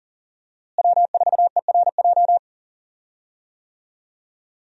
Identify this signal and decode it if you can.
Morse code, heard as a keyed tone, about 38 wpm, 700 Hz. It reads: W4ERJ
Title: USA Call Signs - 5 Characters - Repeated 3x - 40wpm